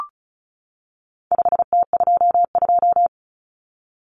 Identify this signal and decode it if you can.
Morse code, heard as a keyed tone, about 34 wpm, 700 Hz. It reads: 5T22
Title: Common Cut Numbers - 4-Digit 35wpm